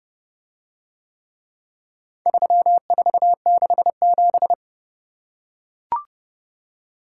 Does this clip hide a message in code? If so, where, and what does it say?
Yes: Morse code from 2.3 to 4.5 s, heard as a keyed tone, about 27 wpm, 700 Hz. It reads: 3467